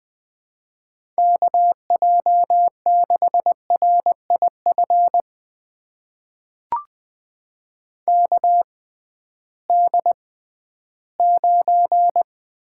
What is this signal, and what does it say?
Morse code, heard as a keyed tone, about 20 wpm, spaced as for 13 wpm, 700 Hz. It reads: KJ6RIF KD9
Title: USA Call Signs - 6 Characters - 6x Character Spacing - 20wpm